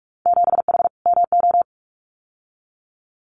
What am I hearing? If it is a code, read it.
Morse code, heard as a keyed tone, about 46 wpm, 690 Hz. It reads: 7H MO
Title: Winter Field Day Exchanges Encore 45wpm